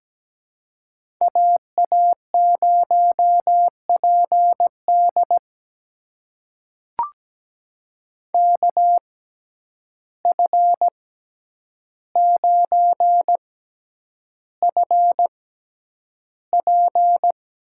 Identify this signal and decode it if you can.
Morse code, heard as a keyed tone, about 17 wpm, spaced as for 9 wpm, 700 Hz. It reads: AA0PD KF9FP
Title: USA Call Signs - 5 Characters - 6x Character Spacing - 17wpm